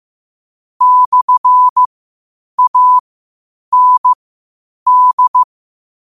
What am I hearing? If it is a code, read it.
Morse code, heard as a keyed tone, about 15 wpm, spaced as for 8 wpm, 1000 Hz. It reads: /AND
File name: qotdfiles / 8wpm